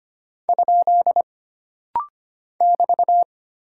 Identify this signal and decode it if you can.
Morse code, heard as a keyed tone, about 25 wpm, 700 Hz. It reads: ? <BT>
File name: Single Punctuation-Prosign - Rapid-Fire - 25wpm